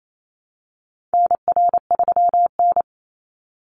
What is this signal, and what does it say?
Morse code, heard as a keyed tone, about 28 wpm, 700 Hz. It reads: NR3N
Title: Call Signs - 28wpm